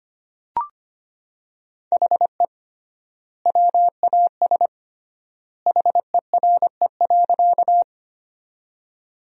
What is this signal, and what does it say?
Morse code, heard as a keyed tone, about 25 wpm, 700 Hz. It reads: HE WAS HERE.
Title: 3x Word-Spacing - Sentences from Top 200 Words - 25wpm